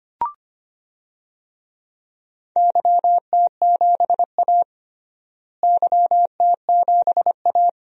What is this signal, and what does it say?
Morse code, heard as a keyed tone, about 25 wpm, 700 Hz. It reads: YT7A YT7A
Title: DXCC-100 Call Signs - 4 Characters - Repeated 3x - 25wpm